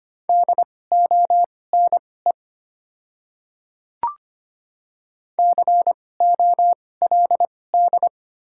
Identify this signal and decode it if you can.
Morse code, heard as a keyed tone, about 25 wpm, spaced as for 18 wpm, 690 Hz. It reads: DONE COLD
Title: 2x Character Spacing - Top 500 Words - 25wpm